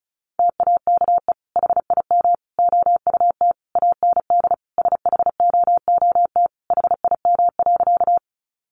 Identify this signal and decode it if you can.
Morse code, heard as a keyed tone, about 34 wpm, 700 Hz. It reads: TAKE HIM OUT AND SHOOT HIM.
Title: Sentences from Top 1200 Words 35wpm